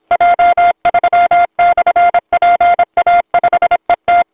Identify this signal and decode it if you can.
Morse code, heard as a keyed tone, about 28 wpm, 700 Hz. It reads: J3/PA5ET
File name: voice / 80m j3 pa5et 002 2001 08 08